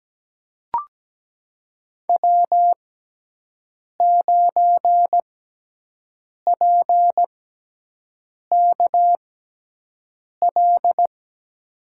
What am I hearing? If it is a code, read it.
Morse code, heard as a keyed tone, about 17 wpm, spaced as for 6 wpm, 700 Hz. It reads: W9PKL
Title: USA Call Signs - 6x Character Spacing - 17wpm